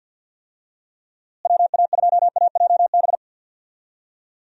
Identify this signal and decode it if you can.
Morse code, heard as a keyed tone, about 52 wpm, 700 Hz. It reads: WA2AJB
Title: USA Call Signs - 6 Characters - Repeated 3x - 50wpm